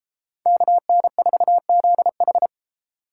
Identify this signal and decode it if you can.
Morse code, heard as a keyed tone, about 30 wpm, 700 Hz. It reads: KN4ZH